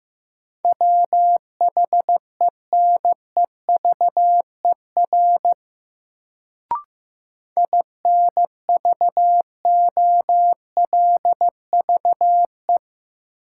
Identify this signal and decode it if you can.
Morse code, heard as a keyed tone, about 15 wpm, 700 Hz. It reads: WHENEVER INVOLVE
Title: Sentences from Top 1200 Words - New Word Drill - 15wpm